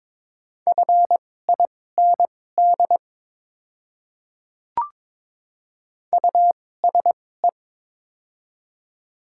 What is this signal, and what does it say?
Morse code, heard as a keyed tone, about 22 wpm, spaced as for 16 wpm, 700 Hz. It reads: FIND USE
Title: 2x Character Spacing - Top 200 Words - 22wpm